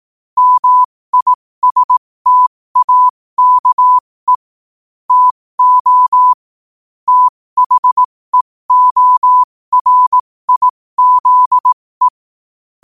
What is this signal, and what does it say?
Morse code, heard as a keyed tone, about 18 wpm, spaced as for 15 wpm, 1000 Hz. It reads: MISTAKE TO THEORIZE
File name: qotdfiles / 15wpm